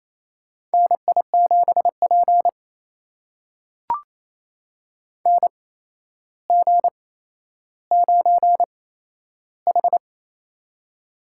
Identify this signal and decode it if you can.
Morse code, heard as a keyed tone, about 28 wpm, spaced as for 12 wpm, 700 Hz. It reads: NI7P NG9H
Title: USA Call Signs - 4 Characters - 8x Character Spacing - 28wpm